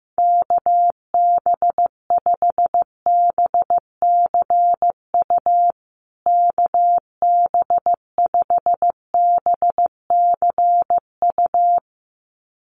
Morse code, heard as a keyed tone, about 15 wpm, 700 Hz. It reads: KB5BCU KB5BCU
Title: USA Call Signs Repeated 2x 15wpm